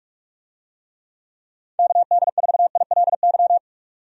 Morse code, heard as a keyed tone, about 46 wpm, 700 Hz. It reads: KD4ILY
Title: USA Call Signs - 6 Characters - 45wpm